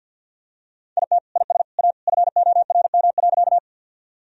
Morse code, heard as a keyed tone, about 51 wpm, 700 Hz. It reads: IT IS A FORM.
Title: Sentences from Top 1000 Words - Review Part 1 - 50wpm